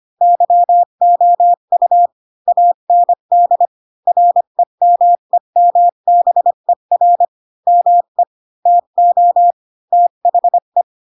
Morse code, heard as a keyed tone, about 25 wpm, 700 Hz. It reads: YOU AND REMEMBER ME TO THE
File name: Pinocchio / Pinocchio ch5 20 25 ik1hge